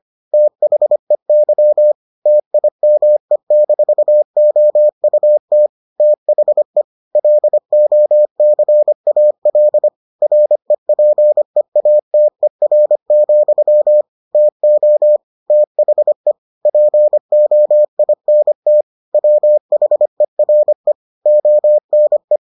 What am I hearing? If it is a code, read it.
Morse code, heard as a keyed tone, about 25 wpm, 600 Hz. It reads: THEY TIME-OUT THE LOCAL REPEATER, TO THE POINT WHERE ONE